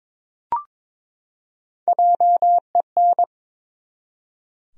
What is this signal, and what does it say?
Morse code, heard as a keyed tone, about 23 wpm, 700 Hz. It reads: JEN
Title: USA Common Names - 22wpm